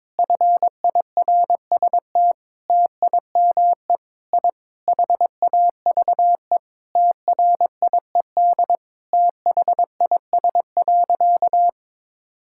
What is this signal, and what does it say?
Morse code, heard as a keyed tone, about 22 wpm, 700 Hz. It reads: FIRST TIME I HAVE TRIED THIS.
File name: practice / 22wpm / Sentences with 4-Word Phrases - 22wpm